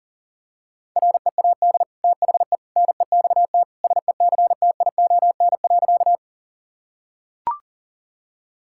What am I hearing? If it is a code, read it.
Morse code, heard as a keyed tone, about 40 wpm, 700 Hz. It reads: READ THE NEXT SECTION.
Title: Sentences from Top 1000 Words - Review Part 5 - 40wpm